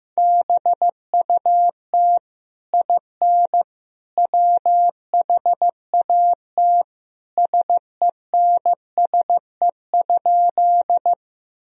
Morse code, heard as a keyed tone, about 15 wpm, 700 Hz. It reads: BUT IN WHAT SENSE?
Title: Sentences from Top 1000 Words - Review Part 4 - 15wpm